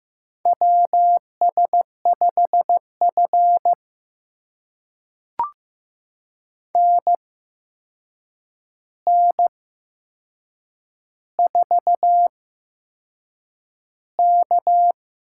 Morse code, heard as a keyed tone, about 15 wpm, spaced as for 6 wpm, 700 Hz. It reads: WS5F NN4K
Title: USA Call Signs - 4 Characters - 8x Character Spacing - 15wpm